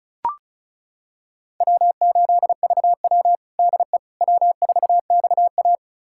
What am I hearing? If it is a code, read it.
Morse code, heard as a keyed tone, about 36 wpm, 700 Hz. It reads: W8VW DE W4XA
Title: USA Callsign DE Callsign - 4 Characters - 35wpm